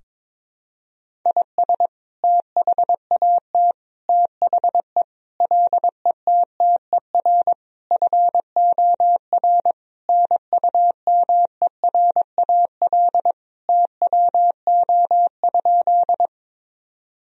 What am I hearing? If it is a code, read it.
Morse code, heard as a keyed tone, about 22 wpm, 700 Hz. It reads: IS THAT THE LETTER FOR NUMERAL TWO?
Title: Sentences from Top 1000 Words - Review Part 2 - 22wpm